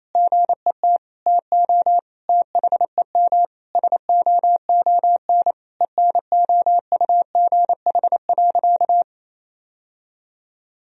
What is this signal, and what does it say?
Morse code, heard as a keyed tone, about 28 wpm, 700 Hz. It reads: GET TO THEM SOON ENOUGH.
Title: Sentences from Top 1000 Words - Review Part 3 - 28wpm